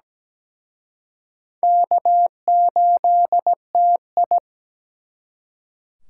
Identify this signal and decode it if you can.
Morse code, heard as a keyed tone, about 21 wpm, 700 Hz. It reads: K8TI
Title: Speed Racing - USA 1x2 Call Signs 17wpm